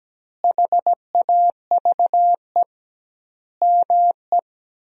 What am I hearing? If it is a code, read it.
Morse code, heard as a keyed tone, about 17 wpm, 700 Hz. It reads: HAVE ME
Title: 2x Word-Spacing - Sentences from Top 200 Words - 17wpm